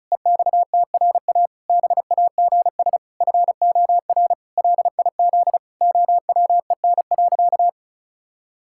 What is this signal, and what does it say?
Morse code, heard as a keyed tone, about 34 wpm, 700 Hz. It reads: EXTRA BAGS FOR LIZ OWEN.
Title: Pangram Challenge v2 - 35wpm